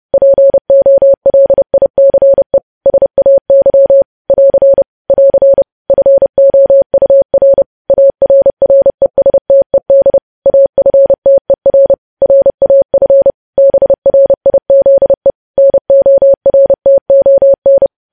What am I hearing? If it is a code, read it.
Morse code, heard as a keyed tone, about 30 wpm, 550 Hz. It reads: POLICE SAY <AR> <AR> FOUR ARRESTED AFTER RAF BRIZE NORTON
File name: podcast / News headlines-1x-30x30